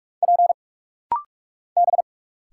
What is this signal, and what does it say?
Morse code, heard as a keyed tone, about 46 wpm, 700 Hz. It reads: P B
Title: Single Letters - Rapid-Fire 45wpm